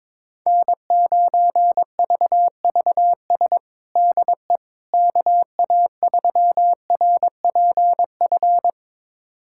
Morse code, heard as a keyed tone, about 22 wpm, 700 Hz. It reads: N9VVS DE KA3RPF
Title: USA Callsign DE Callsign 22wpm